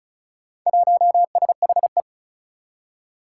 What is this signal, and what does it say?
Morse code, heard as a keyed tone, about 34 wpm, 700 Hz. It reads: 1SHE
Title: Four Challenging Letter-Number Combinations 35wpm